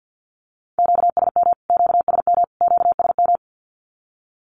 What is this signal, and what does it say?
Morse code, heard as a keyed tone, about 50 wpm, 700 Hz. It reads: QSM QSM QSM
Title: Q-Codes Repeated 3x 50wpm